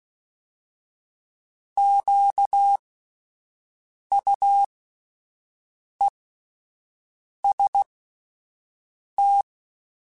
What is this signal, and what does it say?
Morse code, heard as a keyed tone, about 16 wpm, spaced as for 5 wpm, 800 Hz. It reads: QUEST
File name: Alphabetical / Alphabetical mp3s / Qs @5